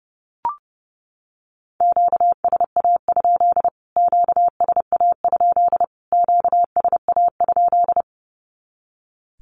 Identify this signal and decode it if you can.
Morse code, heard as a keyed tone, about 30 wpm, 700 Hz. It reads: QSA? QSA? QSA?
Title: Q-Codes Repeated 3x 30wpm